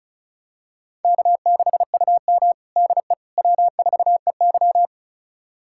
Morse code, heard as a keyed tone, about 36 wpm, 700 Hz. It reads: K6UM DE W4EY
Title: USA Callsign DE Callsign - 4 Characters - 35wpm